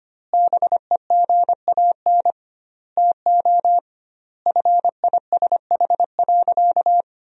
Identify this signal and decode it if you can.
Morse code, heard as a keyed tone, about 25 wpm, 700 Hz. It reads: BEGAN TO FISH.
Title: 2x Word-Spacing - Sentences from Top 300 Words - 25wpm